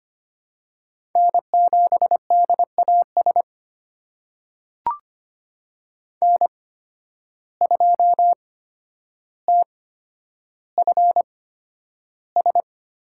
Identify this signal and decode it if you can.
Morse code, heard as a keyed tone, about 25 wpm, spaced as for 11 wpm, 700 Hz. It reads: N7DAS N2TFS
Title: USA Call Signs - 8x Character Spacing - 25wpm